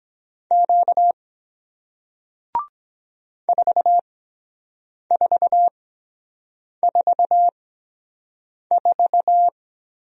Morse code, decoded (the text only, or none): Q 4444